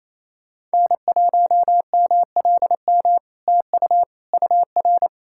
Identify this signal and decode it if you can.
Morse code, heard as a keyed tone, about 28 wpm, 700 Hz. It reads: N1MLM TU UR
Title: SOTA QSO Practice 28wpm